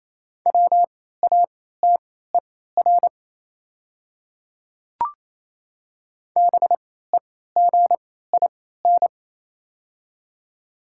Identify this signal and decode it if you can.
Morse code, heard as a keyed tone, about 28 wpm, spaced as for 16 wpm, 700 Hz. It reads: WATER BEGIN
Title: 3x Character Spacing - Top 300 Words - 28wpm